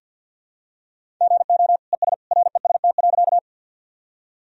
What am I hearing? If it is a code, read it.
Morse code, heard as a keyed tone, about 52 wpm, 700 Hz. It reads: GO ES REST.
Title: Sentences from Top 500 Words and Abbreviations 50wpm